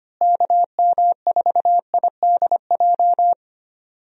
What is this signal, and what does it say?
Morse code, heard as a keyed tone, about 25 wpm, 700 Hz. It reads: KM4IDJ
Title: USA Call Signs - 6 Characters - Repeated 3x - 25wpm